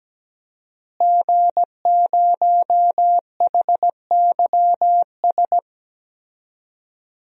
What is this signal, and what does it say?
Morse code, heard as a keyed tone, about 17 wpm, 700 Hz. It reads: G0HYS
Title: DXCC-100 Call Signs - 5 Characters - Repeated 3x - 17wpm